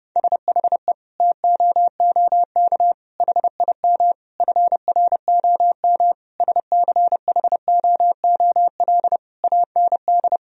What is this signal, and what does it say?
Morse code, heard as a keyed tone, about 30 wpm, 700 Hz. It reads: SHE TOOK HIM FROM SCHOOL AND
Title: Sentences from Top 1000 Words - Review Part 4 - 30wpm